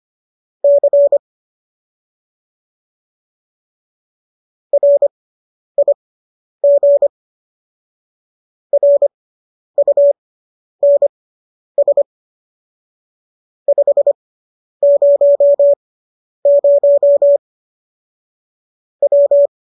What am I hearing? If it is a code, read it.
Morse code, heard as a keyed tone, about 25 wpm, spaced as for 10 wpm, 570 Hz. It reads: C RIG RUNS 500 W
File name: qso219 10